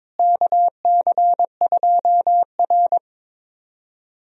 Morse code, heard as a keyed tone, about 22 wpm, 700 Hz. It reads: KC2R
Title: USA Call Signs - 4 Characters - Repeated 2x - 22wpm